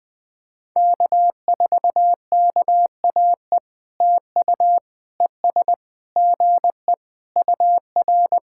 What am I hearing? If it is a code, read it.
Morse code, heard as a keyed tone, about 19 wpm, 700 Hz. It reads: K4KAE TU ES GE UR
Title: POTA QSO Practice - 20wpm